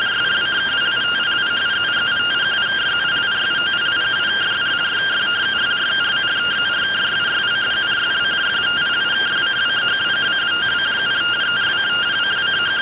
Two-tone teleprinter signal, about 170 Hz shift.